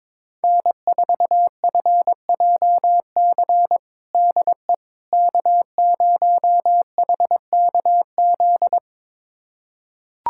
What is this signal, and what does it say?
Morse code, heard as a keyed tone, about 22 wpm, 700 Hz. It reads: N4FJC DE K0HKZ
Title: USA Callsign DE Callsign - 5 Characters - 22wpm